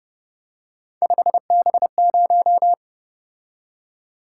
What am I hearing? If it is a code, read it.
Morse code, heard as a keyed tone, about 30 wpm, 700 Hz. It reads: 5B0